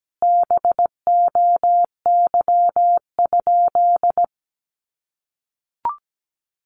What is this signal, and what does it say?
Morse code, heard as a keyed tone, about 17 wpm, 700 Hz. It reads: BOY?
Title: Sentences from Top 1000 Words 17wpm